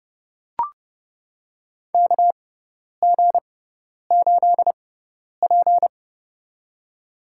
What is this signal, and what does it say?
Morse code, heard as a keyed tone, about 30 wpm, spaced as for 10 wpm, 700 Hz. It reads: KG8P
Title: USA Call Signs - 4 Characters - 6x Character Spacing - 30wpm